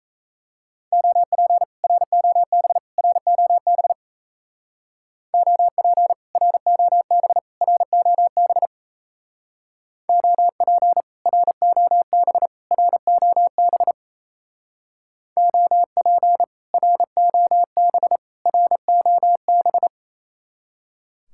Morse code, heard as a keyed tone, about 33 wpm, 700 Hz. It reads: OP ROB ROB OP ROB ROB OP ROB ROB OP ROB ROB
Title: Speed-Racing Common QSO Phrases - 28wpm